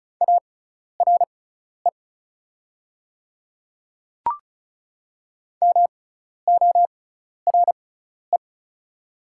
Morse code, heard as a keyed tone, about 36 wpm, spaced as for 12 wpm, 700 Hz. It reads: ARE MORE
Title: 6x Character Spacing - Top 200 Words - 35wpm